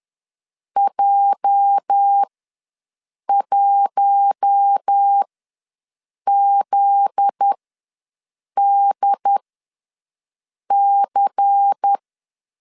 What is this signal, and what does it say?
Morse code, heard as a keyed tone, about 10 wpm, spaced as for 6 wpm, 800 Hz. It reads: J1ZDC
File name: LEZIONI / 8 YJ45 / 8 YJ ws 10wpm 4043